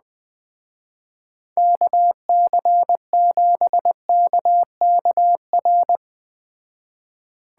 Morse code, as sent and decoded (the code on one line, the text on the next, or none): -.- -.-. --... -.- -.- .-.
KC7KKR